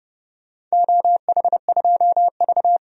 Morse code, heard as a keyed tone, about 30 wpm, 700 Hz. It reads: OH2V
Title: DXCC-100 Call Signs - 4 Characters - 30wpm